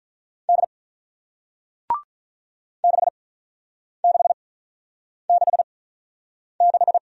Morse code, heard as a keyed tone, about 47 wpm, spaced as for 9 wpm, 700 Hz. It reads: D 6666